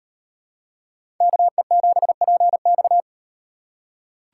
Morse code, heard as a keyed tone, about 38 wpm, 700 Hz. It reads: KE7PX